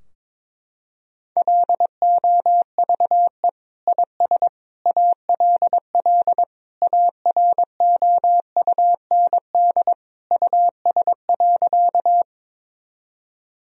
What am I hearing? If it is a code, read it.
Morse code, heard as a keyed tone, about 22 wpm, 700 Hz. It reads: LOVE IS ALL AROUND US.
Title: Sentences from Top 1000 Words - Review Part 3 - 22wpm